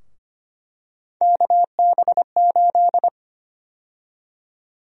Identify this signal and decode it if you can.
Morse code, heard as a keyed tone, about 25 wpm, 700 Hz. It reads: KB8
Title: USA Call Sign Prefixes - Repeated 2x - 25wpm